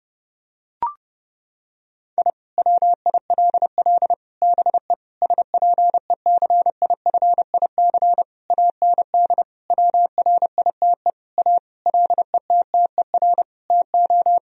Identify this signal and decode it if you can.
Morse code, heard as a keyed tone, about 30 wpm, 700 Hz. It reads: I WILL BE SPECIFIC AND WRITE A LETTER TO
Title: Sentences from Top 1000 Words - Review Part 3 - 30wpm